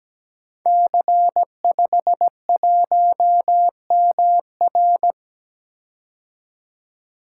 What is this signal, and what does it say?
Morse code, heard as a keyed tone, about 17 wpm, 700 Hz. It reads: C51MR
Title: Code Groups - 5 Letter:Numbers - 17wpm